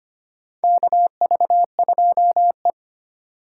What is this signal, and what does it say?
Morse code, heard as a keyed tone, about 25 wpm, 700 Hz. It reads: KV2E
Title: USA Call Signs - 4 Characters - 2x Character Spacing - 25wpm